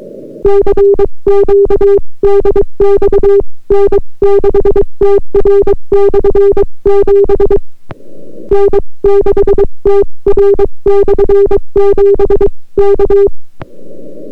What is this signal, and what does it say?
Morse code, heard as a keyed tone, about 22 wpm, 400 Hz. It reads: CQ DX N6TR/7 N6TR/7 K